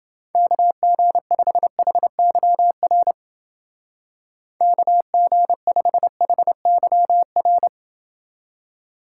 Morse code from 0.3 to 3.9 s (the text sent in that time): KG5HYR